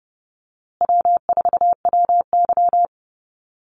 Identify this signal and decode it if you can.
Morse code, heard as a keyed tone, about 30 wpm, 700 Hz. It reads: W4WY
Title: USA Call Signs Repeated 3x 30wpm